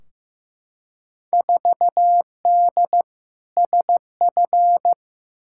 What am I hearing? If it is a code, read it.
Morse code, heard as a keyed tone, about 15 wpm, 700 Hz. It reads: 4D SF
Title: Field Day Exchanges Encore - 15wpm